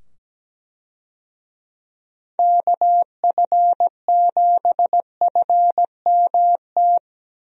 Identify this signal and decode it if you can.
Morse code, heard as a keyed tone, about 17 wpm, 700 Hz. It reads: KF7FMT
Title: USA Call Signs - 6 Characters - 17wpm